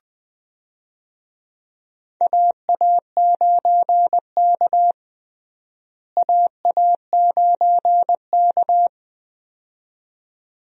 Morse code, heard as a keyed tone, about 21 wpm, 700 Hz. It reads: AA9K AA9K
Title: USA Call Signs - 4 Characters - Repeated 2x - 20wpm